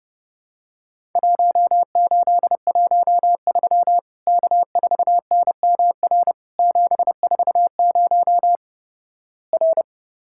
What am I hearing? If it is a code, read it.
Morse code, heard as a keyed tone, about 30 wpm, 700 Hz. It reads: 1813 K4NMR 740 R
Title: Worked All Europe DX Contest - QTC Exchange - 30wpm